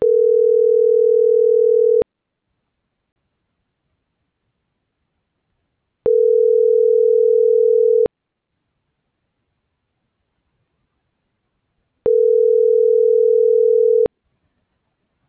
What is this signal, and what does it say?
Ringback tone (North America), 440 + 480 Hz.